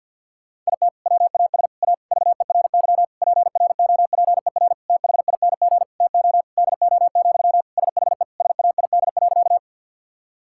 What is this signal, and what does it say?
Morse code, heard as a keyed tone, about 47 wpm, 700 Hz. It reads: IT WAS A VERY PROPER THING TO DO, SHE SAID.
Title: Sentences from Top 1000 Words - Review Part 5 - 50wpm